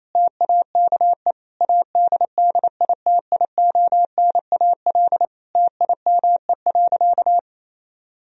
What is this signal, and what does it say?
Morse code, heard as a keyed tone, about 28 wpm, 700 Hz. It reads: TAKE ADDITIONAL TIME.
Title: Sentences from Top 1000 Words - Review Part 4 - 28wpm